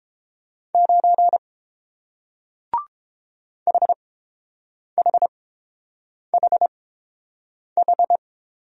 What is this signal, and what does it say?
Morse code, heard as a keyed tone, about 30 wpm, spaced as for 7 wpm, 700 Hz. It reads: 9 HHHH